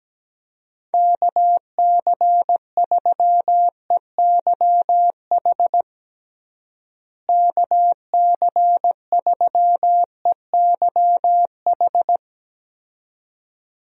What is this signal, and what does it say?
Morse code, heard as a keyed tone, about 17 wpm, 700 Hz. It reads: KC3EYH KC3EYH
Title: USA Call Signs - 6 Characters - Repeated 2x - 17wpm